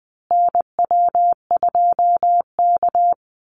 Morse code, heard as a keyed tone, about 19 wpm, 700 Hz. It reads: NW2K
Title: Call Signs Repeated 2x - 20wpm